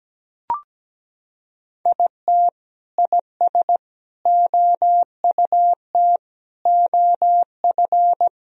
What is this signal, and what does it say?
Morse code, heard as a keyed tone, about 17 wpm, 700 Hz. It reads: IT IS OUT OF
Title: Sentences from Top 1000 Words - Review Part 4 - 17wpm